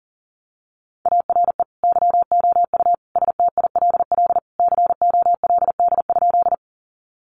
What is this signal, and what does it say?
Morse code, heard as a keyed tone, about 40 wpm, 700 Hz. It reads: ARE YOU STILL COLD?
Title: Sentences from Top 400 Words 40wpm